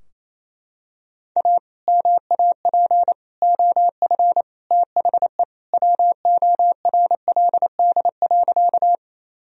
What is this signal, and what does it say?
Morse code, heard as a keyed tone, about 28 wpm, 700 Hz. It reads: A MAP OF THE WORLD.
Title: Sentences from Top 1000 Words - Review Part 2 - 28wpm